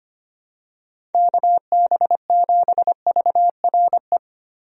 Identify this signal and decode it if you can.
Morse code, heard as a keyed tone, about 25 wpm, 700 Hz. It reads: KB7VRE
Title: USA Call Signs - 6 Characters - 25wpm